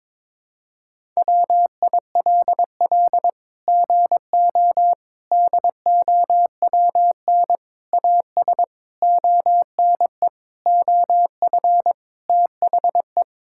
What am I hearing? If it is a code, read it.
Morse code, heard as a keyed tone, about 22 wpm, 700 Hz. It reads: WILL GO DOWN AS ONE OF THE
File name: practice / 22wpm / Sets of 7 Words - 22wpm